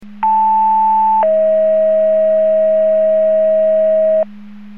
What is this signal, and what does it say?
Two-tone page, heard as a one-second tone, then a three-second tone, about 880 then 640 Hz, with noise. It is heard from 0.2 to 4.2 s.
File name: Portage / 00415 05-43 Portage 154.385 Paging